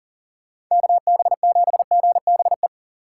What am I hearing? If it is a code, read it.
Morse code, heard as a keyed tone, about 37 wpm, 700 Hz. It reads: KB7GBE
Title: USA Call Signs - 6 Characters - 8x Character Spacing - 40wpm